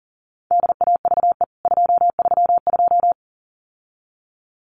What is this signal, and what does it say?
Morse code, heard as a keyed tone, about 40 wpm, 700 Hz. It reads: DAVE 232
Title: CWT Exchange 40wpm